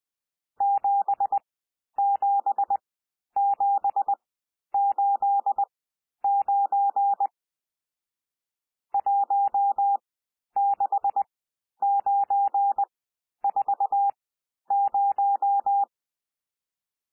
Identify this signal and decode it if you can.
Morse code, heard as a keyed tone, about 20 wpm, spaced as for 11 wpm, 800 Hz. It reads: 77789 16940
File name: cwmp3 / N1 SOUND NUMBERS 20 wpm